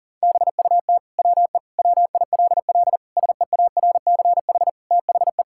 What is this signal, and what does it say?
Morse code, heard as a keyed tone, about 40 wpm, 700 Hz. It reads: BUT WE WILL SEARCH THE
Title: Sentences from Top 1000 Words - Review Part 5 - 40wpm